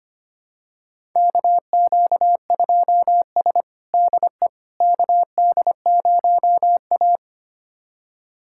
Morse code, heard as a keyed tone, about 25 wpm, 700 Hz. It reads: KQ2S DE KD0A
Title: USA Callsign DE Callsign - 4 Characters - 25wpm